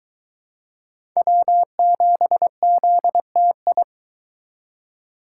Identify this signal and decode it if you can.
Morse code, heard as a keyed tone, about 24 wpm, 700 Hz. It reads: W7ZTI